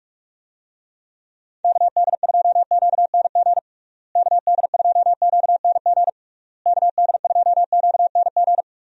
Morse code, heard as a keyed tone, about 46 wpm, 700 Hz. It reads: KD2QNG KD2QNG KD2QNG
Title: USA Call Signs - 6 Characters - Repeated 3x - 45wpm